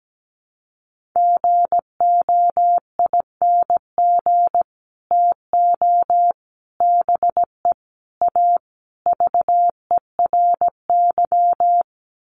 Morse code, heard as a keyed tone, about 17 wpm, 700 Hz. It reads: GOING TO BE A VERY
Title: Sets of 5 Words Encore 17wpm